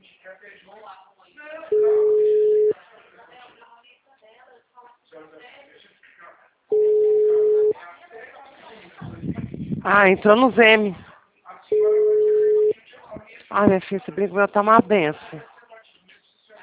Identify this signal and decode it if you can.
Ringback tone (Europe, 425 Hz), with speech.